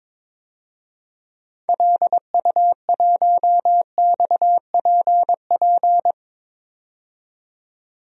Morse code, heard as a keyed tone, about 22 wpm, 700 Hz. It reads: LU1XPP
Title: DXCC-100 Call Signs - 6 Characters - 22wpm